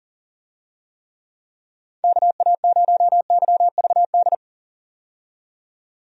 Morse code, heard as a keyed tone, about 40 wpm, 700 Hz. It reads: KA0YVD
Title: USA Call Signs - 6 Characters - 40wpm